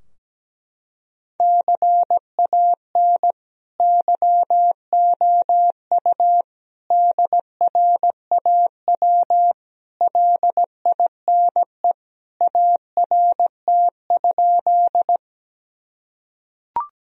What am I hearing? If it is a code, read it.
Morse code, heard as a keyed tone, about 17 wpm, 700 Hz. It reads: CAN YOU DRAW LINE ART?
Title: Sentences from Top 1000 Words - Review Part 3 - 17wpm